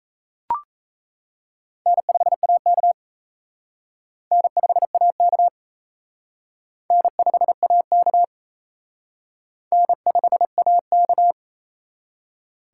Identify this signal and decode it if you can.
Morse code, heard as a keyed tone, about 35 wpm, 700 Hz. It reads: N5AK N5AK N5AK N5AK
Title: Speed Racing - USA 1x2 Call Signs 28wpm